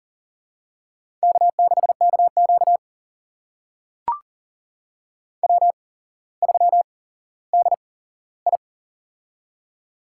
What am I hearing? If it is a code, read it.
Morse code, heard as a keyed tone, about 40 wpm, spaced as for 17 wpm, 700 Hz. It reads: K6KQ W3DI
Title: USA Call Signs - 8x Character Spacing - 40wpm